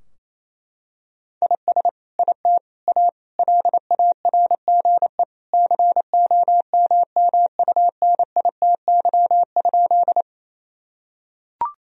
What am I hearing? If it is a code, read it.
Morse code, heard as a keyed tone, about 28 wpm, 700 Hz. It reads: IS IT A LARGE COMMUNITY?
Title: Sentences from Top 1000 Words - Review Part 4 - 28wpm